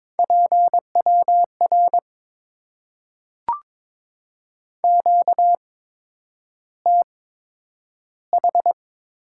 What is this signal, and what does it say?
Morse code, heard as a keyed tone, about 22 wpm, spaced as for 9 wpm, 700 Hz. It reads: PWR QTH
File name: Top 90 QSO Elements - 8x Character Spacing - 22wpm